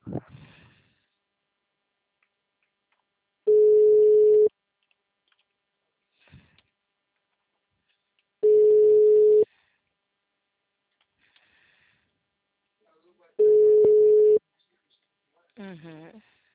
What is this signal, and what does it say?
Ringback tone (Europe, 425 Hz), with speech.